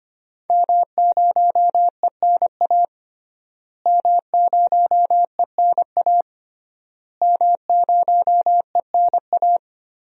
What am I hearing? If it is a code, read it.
Morse code, heard as a keyed tone, about 25 wpm, 700 Hz. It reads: M0ENA M0ENA M0ENA
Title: DXCC-100 Call Signs - 5 Characters - Repeated 3x - 25wpm